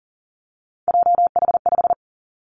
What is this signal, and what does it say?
Morse code, heard as a keyed tone, about 42 wpm, 700 Hz. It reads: JH5
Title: Three Challenging Letter-Number Combinations 40wpm